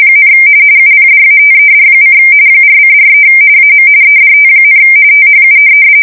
RTTY, about 45.45 baud. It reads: , UPDATED: 4:43 AM EDT, THU
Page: 170 Shift 2125/2295